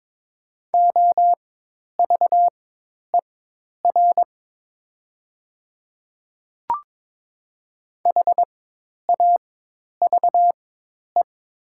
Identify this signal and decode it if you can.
Morse code, heard as a keyed tone, about 22 wpm, spaced as for 10 wpm, 700 Hz. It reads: OVER HAVE
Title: 4x Character Spacing - Top 100 Words - 22wpm